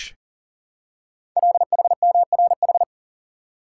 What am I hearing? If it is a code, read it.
Morse code, heard as a keyed tone, about 42 wpm, 690 Hz. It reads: LHMRH
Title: Code Groups - 5 Letters - 40wpm